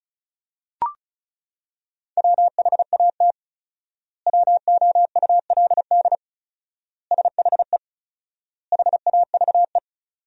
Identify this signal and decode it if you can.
Morse code, heard as a keyed tone, about 36 wpm, 700 Hz. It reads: WHAT WOULD SHE HAVE
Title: 4x Word-Spacing - Sentences from Top 100 Words - 35wpm